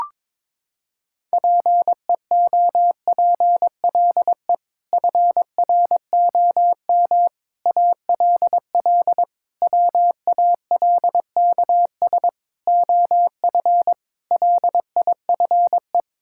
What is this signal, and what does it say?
Morse code, heard as a keyed tone, about 22 wpm, 700 Hz. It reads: PEOPLE FROM ALL WALKS OF LIFE
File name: practice / 22wpm / Sets of 6 Words - 22wpm